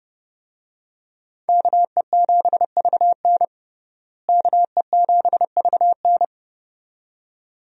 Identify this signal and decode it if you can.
Morse code, heard as a keyed tone, about 30 wpm, 700 Hz. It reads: KE7VN KE7VN
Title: USA Call Signs - 5 Characters - Repeated 2x - 30wpm